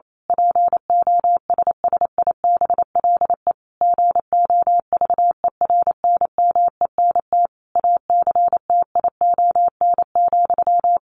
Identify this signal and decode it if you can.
Morse code, heard as a keyed tone, about 28 wpm, 700 Hz. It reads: POSSIBLE GOVERNMENT ACTION,
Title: A Sound Of Thunder 28wpm